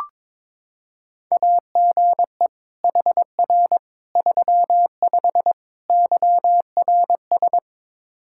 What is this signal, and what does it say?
Morse code, heard as a keyed tone, about 22 wpm, 700 Hz. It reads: AGE HR 35 YRS
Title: Practice QSO - Exchange 3 - More Info - 22wpm